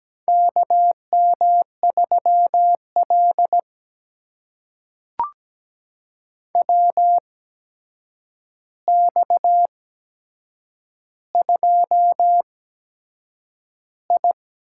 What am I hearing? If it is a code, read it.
Morse code, heard as a keyed tone, about 17 wpm, spaced as for 7 wpm, 700 Hz. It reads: KM3L WX2I
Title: USA Call Signs - 4 Characters - 8x Character Spacing - 17wpm